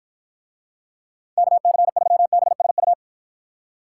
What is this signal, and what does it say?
Morse code, heard as a keyed tone, about 51 wpm, 700 Hz. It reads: KC3BSU